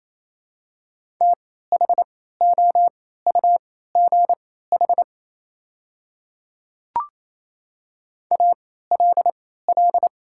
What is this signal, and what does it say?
Morse code, heard as a keyed tone, about 28 wpm, spaced as for 16 wpm, 690 Hz. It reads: THOUGH ALL
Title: 3x Character Spacing - Top 500 Words - 28wpm